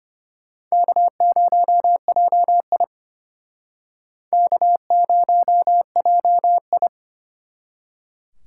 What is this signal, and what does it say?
Morse code, heard as a keyed tone, about 33 wpm, 700 Hz. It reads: K0JI K0JI